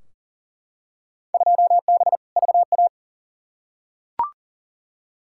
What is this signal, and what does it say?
Morse code, heard as a keyed tone, about 40 wpm, 700 Hz. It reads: 2B VA
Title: Field Day Exchanges Encore - 40wpm